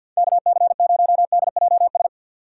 Morse code, heard as a keyed tone, about 53 wpm, 700 Hz. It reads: K C 0 D J S
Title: USA Call Signs - 6 Characters - 6x Character Spacing - 50wpm